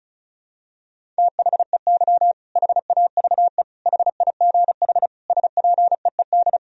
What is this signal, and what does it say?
Morse code, heard as a keyed tone, about 36 wpm, 700 Hz. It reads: THEY HAVE HIGH SPEED
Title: Sentences from Top 1000 Words - Review Part 4 - 35wpm